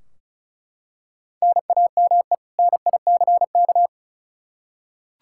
Morse code, heard as a keyed tone, about 34 wpm, 700 Hz. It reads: NAME NICK
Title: Practice QSO - Exchange 1 - Beginning - 35wpm